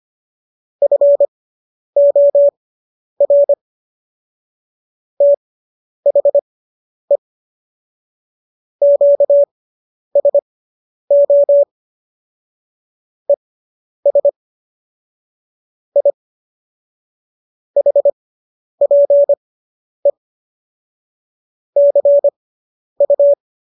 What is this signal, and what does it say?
Morse code, heard as a keyed tone, about 25 wpm, spaced as for 10 wpm, 570 Hz. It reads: FOR THE QSO ES I HPE CU